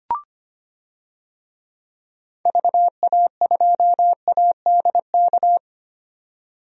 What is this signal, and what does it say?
Morse code, heard as a keyed tone, about 25 wpm, 700 Hz. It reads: VA2ADK
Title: DXCC-100 Call Signs - 6 Characters - 25wpm